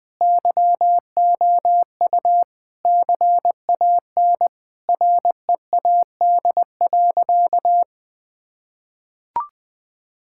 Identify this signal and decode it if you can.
Morse code, heard as a keyed tone, about 20 wpm, 700 Hz. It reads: YOU CAN READ.
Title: Sentences with 8-Word Phrases - 20wpm